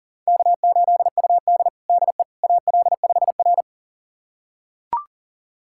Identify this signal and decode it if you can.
Morse code, heard as a keyed tone, about 40 wpm, 700 Hz. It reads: K8UD DE AL5R
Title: USA Callsign DE Callsign - 4 Characters - 40wpm